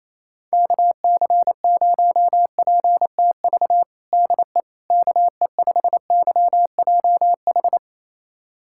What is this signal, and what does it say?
Morse code, heard as a keyed tone, about 28 wpm, 700 Hz. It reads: KC0PTV DE KE5YJH
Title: USA Callsign DE Callsign 28wpm